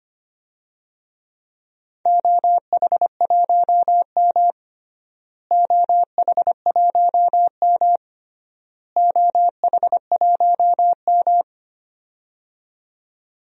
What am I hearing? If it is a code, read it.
Morse code, heard as a keyed tone, about 25 wpm, 700 Hz. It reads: OH1M OH1M OH1M
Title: DXCC-100 Call Signs - 4 Characters - Repeated 3x - 25wpm